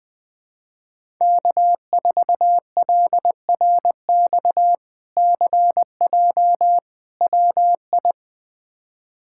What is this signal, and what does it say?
Morse code, heard as a keyed tone, about 20 wpm, 700 Hz. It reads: K4LRX CJ WI
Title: K1USN SST QSO Practice - 20wpm